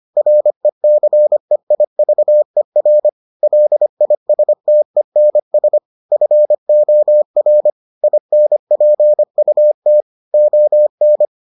Morse code, heard as a keyed tone, about 25 wpm, 600 Hz. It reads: RECEIVER LISTENS FOR INPUT ON